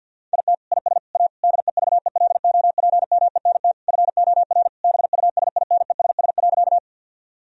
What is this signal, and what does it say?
Morse code, heard as a keyed tone, about 51 wpm, 700 Hz. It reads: IT IS A DEVELOPMENT FOR BUSINESS.
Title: Sentences from Top 1000 Words - Review Part 4 - 50wpm